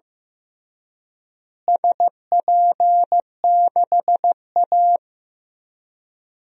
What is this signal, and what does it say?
Morse code, heard as a keyed tone, about 16 wpm, 700 Hz. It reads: SP6A
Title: DXCC-100 Call Signs - 4 Characters - Repeated 2x - 15wpm